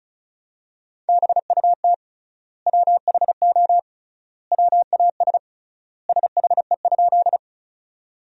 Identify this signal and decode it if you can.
Morse code, heard as a keyed tone, about 36 wpm, 700 Hz. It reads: BUT WHO WAS SHE?
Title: 3x Word-Spacing - Sentences from Top 100 Words - 35wpm